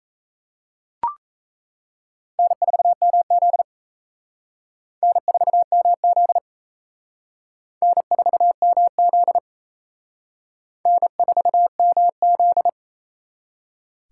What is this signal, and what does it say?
Morse code, heard as a keyed tone, about 34 wpm, 700 Hz. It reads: N4MZ N4MZ N4MZ N4MZ
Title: Speed Racing - USA 1x2 Call Signs 28wpm